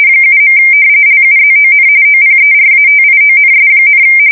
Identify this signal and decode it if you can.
RTTY, about 45.45 baud. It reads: 7. LUZON OPERATIONS,
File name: fsk170 / battleshipnj